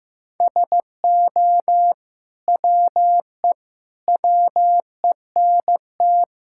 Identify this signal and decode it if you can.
Morse code, heard as a keyed tone, about 16 wpm, 700 Hz. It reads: SO WE WENT
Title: Sentences from Top 1000 Words - Review Part 5 - 15wpm